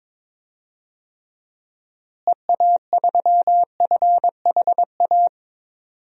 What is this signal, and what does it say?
Morse code, heard as a keyed tone, about 22 wpm, 700 Hz. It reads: EA3FHA
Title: DXCC-100 Call Signs - 6 Characters - Repeated 3x - 22wpm